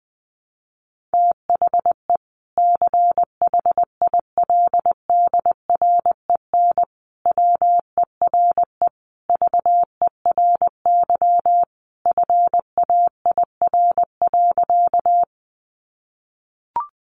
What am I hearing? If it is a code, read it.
Morse code, heard as a keyed tone, about 20 wpm, 700 Hz. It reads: THE CHILDREN WERE VERY FAIR.
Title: Sentences from Top 1300 Words 20wpm